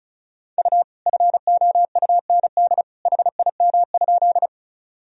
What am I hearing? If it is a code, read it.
Morse code, heard as a keyed tone, about 34 wpm, 700 Hz. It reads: U FOUND HIM?
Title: Sentences from Top 300 Words and Abbreviations 35wpm